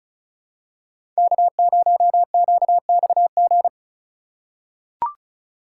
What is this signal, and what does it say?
Morse code, heard as a keyed tone, about 36 wpm, 700 Hz. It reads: K0QXG
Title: USA Call Signs - 5 Characters - 2x Character Spacing - 35wpm